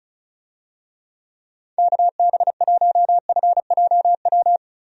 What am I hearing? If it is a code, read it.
Morse code, heard as a keyed tone, about 34 wpm, 700 Hz. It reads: KB1FJW
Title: USA Call Signs - 6 Characters - 35wpm